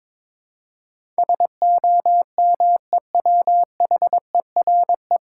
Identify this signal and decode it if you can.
Morse code, heard as a keyed tone, about 22 wpm, 700 Hz. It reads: SOMEWHERE
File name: 9 Letter Words - First 3 Letters Given - 22wpm